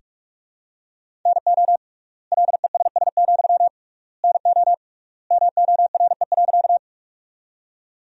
Morse code, heard as a keyed tone, about 44 wpm, 700 Hz. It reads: NO LESS, NO MORE.
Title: 3x Word-Spacing - Sentences from Top 400 Words - 45wpm